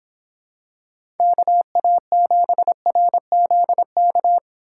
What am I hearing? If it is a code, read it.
Morse code, heard as a keyed tone, about 26 wpm, 700 Hz. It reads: KA7RZK